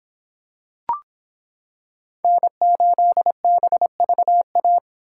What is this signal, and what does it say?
Morse code, heard as a keyed tone, about 25 wpm, 700 Hz. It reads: N8BVA